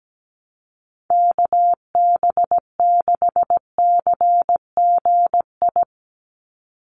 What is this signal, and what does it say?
Morse code, heard as a keyed tone, about 17 wpm, 700 Hz. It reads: KB6CGI
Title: USA Call Signs 17wpm